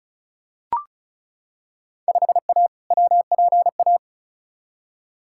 Morse code, heard as a keyed tone, about 36 wpm, 700 Hz. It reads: 5A WPA
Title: Field Day Exchanges Encore - 35wpm